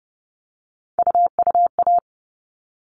Morse code, heard as a keyed tone, about 30 wpm, 700 Hz. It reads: UUA
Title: All Cut Numbers - 3-Digit 30wpm